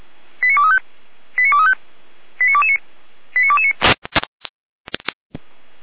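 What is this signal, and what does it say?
Five-tone selective call (ZVEI), heard as quick runs of five tones, tones of about 70 ms. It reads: 89235 89235 89229 89229
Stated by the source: ZVEI five-tone alert received by SDR, fire brigade feuerwehr-orlamuende.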